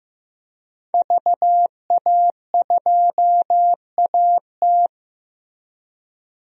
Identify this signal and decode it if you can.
Morse code, heard as a keyed tone, about 15 wpm, 700 Hz. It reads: VA2AT
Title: DXCC-100 Call Signs - 5 Characters - Repeated 2x - 15wpm